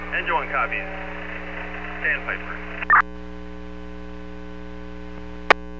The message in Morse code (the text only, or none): none